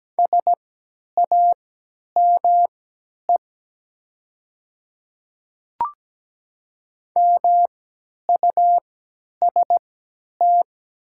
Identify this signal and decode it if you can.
Morse code, heard as a keyed tone, about 17 wpm, spaced as for 10 wpm, 700 Hz. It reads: SAME MUST
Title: 3x Character Spacing - Top 300 Words - 17wpm